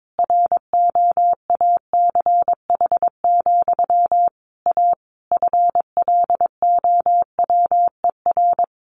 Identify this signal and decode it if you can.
Morse code, heard as a keyed tone, about 22 wpm, 700 Hz. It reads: ROACH, A FLOWER
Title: A Sound Of Thunder 22wpm